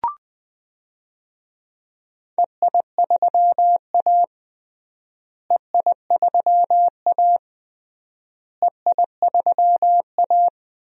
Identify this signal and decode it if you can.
Morse code, heard as a keyed tone, about 20 wpm, 700 Hz. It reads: EI3A EI3A EI3A
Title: DXCC-100 Call Signs - 4 Characters - Repeated 3x - 20wpm